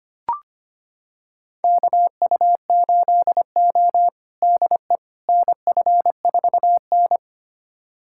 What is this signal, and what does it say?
Morse code, heard as a keyed tone, about 25 wpm, 700 Hz. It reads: KU8O DE NF4N
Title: USA Callsign DE Callsign - 4 Characters - 25wpm